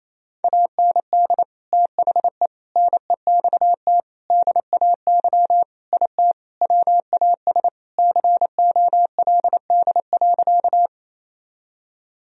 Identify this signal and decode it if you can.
Morse code, heard as a keyed tone, about 28 wpm, 700 Hz. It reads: AND THE NEXT DAY IT WAS COLD.
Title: Sentences from Top 1000 Words - Review Part 1 - 28wpm